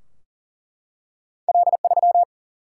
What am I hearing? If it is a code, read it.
Morse code, heard as a keyed tone, about 40 wpm, 700 Hz. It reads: L 3
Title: Single-Letter Single-Number 40wpm